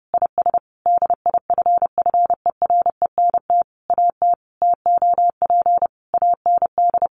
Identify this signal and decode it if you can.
Morse code, heard as a keyed tone, about 30 wpm, 700 Hz. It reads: IS DIFFERENT AT TOP AND
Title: Sentences from Top 1400 Words 30wpm